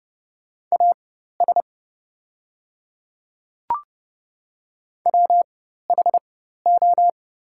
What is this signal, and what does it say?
Morse code, heard as a keyed tone, about 29 wpm, spaced as for 14 wpm, 700 Hz. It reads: AS WHO
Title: 4x Character Spacing - Top 100 Words - 30wpm